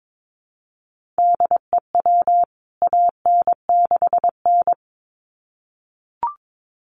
Morse code, heard as a keyed tone, about 22 wpm, 700 Hz. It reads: DEW AN6N
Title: CWT Exchange 22wpm